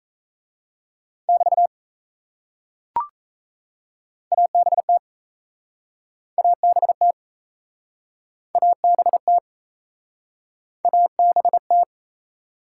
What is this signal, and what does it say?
Morse code, heard as a keyed tone, about 35 wpm, 700 Hz. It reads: <BT> ABT ABT ABT ABT